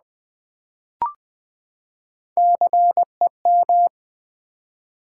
Morse code, heard as a keyed tone, about 20 wpm, 700 Hz. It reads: CEM
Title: Build A Word - 3 Letters At A Time - 8-Letter Words - 20wpm